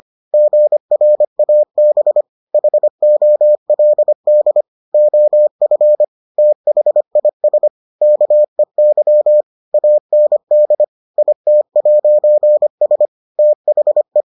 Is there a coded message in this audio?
Yes: Morse code from 0.3 to 14.2 s, heard as a keyed tone, about 25 wpm, 600 Hz. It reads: GRAB HOLD OF THIS KEY AND IT'S THE